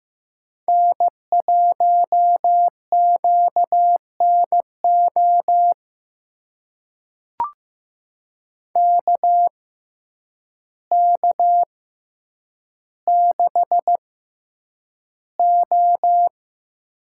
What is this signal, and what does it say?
Morse code, heard as a keyed tone, about 15 wpm, spaced as for 8 wpm, 700 Hz. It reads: N1QNO KK6O
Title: USA Call Signs - 6x Character Spacing - 15wpm